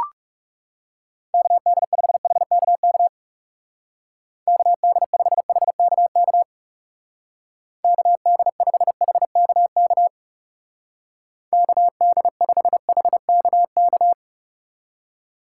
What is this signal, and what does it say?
Morse code, heard as a keyed tone, about 37 wpm, 700 Hz. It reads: KD5HKK KD5HKK KD5HKK KD5HKK
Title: Speed Racing - USA 2x3 Call Signs 30wpm